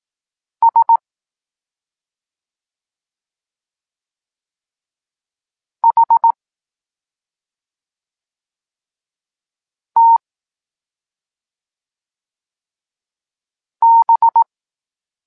Morse code, heard as a keyed tone, about 18 wpm, 920 Hz. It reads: S H T B
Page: Tekst 14 – doskonalenie odbioru poznanych liter